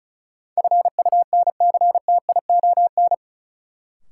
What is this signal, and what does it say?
Morse code, heard as a keyed tone, about 34 wpm, 700 Hz. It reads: FUNCTION
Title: Sentences from Top 1100 Words - New Word Drill Alt - 35wpm